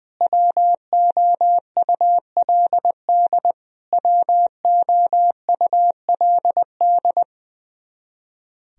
Morse code, heard as a keyed tone, about 21 wpm, 700 Hz. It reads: WOULD WOULD
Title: Top 100 Words Repeated 20wpm